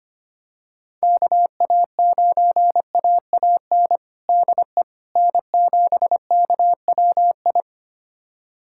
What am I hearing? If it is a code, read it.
Morse code, heard as a keyed tone, about 25 wpm, 700 Hz. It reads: KA9AAN DE N7KWI
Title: USA Callsign DE Callsign 25wpm